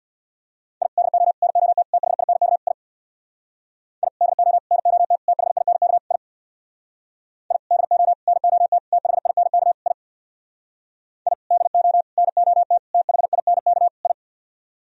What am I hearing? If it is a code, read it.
Morse code, heard as a keyed tone, about 64 wpm, 700 Hz. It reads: I DO NOT THINK I I DO NOT THINK I I DO NOT THINK I I DO NOT THINK I